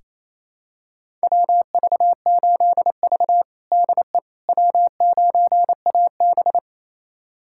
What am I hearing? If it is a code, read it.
Morse code, heard as a keyed tone, about 27 wpm, 700 Hz. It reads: WV8V DE W9AB
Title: USA Callsign DE Callsign - 4 Characters - 28wpm